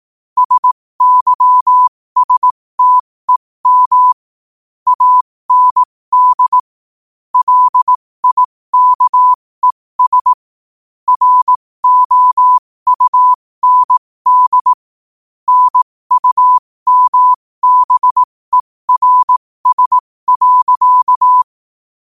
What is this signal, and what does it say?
Morse code, heard as a keyed tone, about 18 wpm, spaced as for 15 wpm, 1000 Hz. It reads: SYSTEM AND LIKES ROUND NUMBERS.
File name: qotdfiles / 15wpm